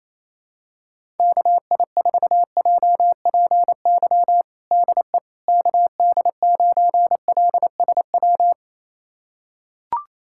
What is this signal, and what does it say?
Morse code, heard as a keyed tone, about 28 wpm, 700 Hz. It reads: KI4JPY DE KD9LSW
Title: USA Callsign DE Callsign 28wpm